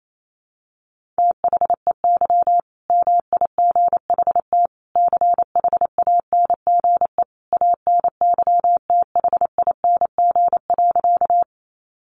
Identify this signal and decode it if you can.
Morse code, heard as a keyed tone, about 28 wpm, 700 Hz. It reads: THEY MIGHT CHANGE ANYTHING.
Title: A Sound Of Thunder 28wpm